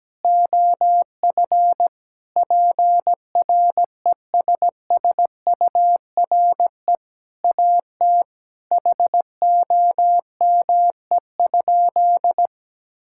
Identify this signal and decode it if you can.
Morse code, heard as a keyed tone, about 17 wpm, 700 Hz. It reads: OF PRESSURE AT HOME?
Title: Sentences from Top 1000 Words - Review Part 6 - 17wpm